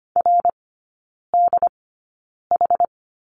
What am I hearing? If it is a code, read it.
Morse code, heard as a keyed tone, about 25 wpm, spaced as for 9 wpm, 700 Hz. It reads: RDH
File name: course / 25:10wpm / 093 Lesson 16e - Letter c - Three Characters 25wpm